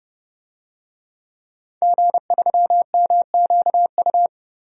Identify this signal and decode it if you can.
Morse code, heard as a keyed tone, about 30 wpm, 700 Hz. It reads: G3MQU
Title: DXCC-100 Call Signs - 5 Characters - 30wpm